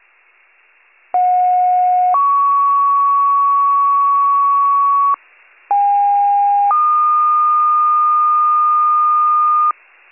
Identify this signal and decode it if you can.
Two-tone page, heard as a one-second tone, then a three-second tone, about 730 then 1120 Hz, about 810 then 1220 Hz. It is heard from 1.1 to 9.7 s.